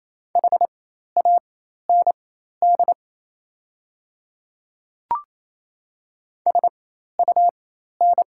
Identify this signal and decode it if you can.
Morse code, heard as a keyed tone, about 28 wpm, spaced as for 13 wpm, 700 Hz. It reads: HAND SUN
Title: 4x Character Spacing - Top 300 Words - 28wpm